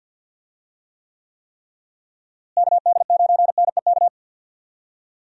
Morse code, heard as a keyed tone, about 49 wpm, 700 Hz. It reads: KD9NEK
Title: USA Call Signs - 6 Characters - Repeated 3x - 50wpm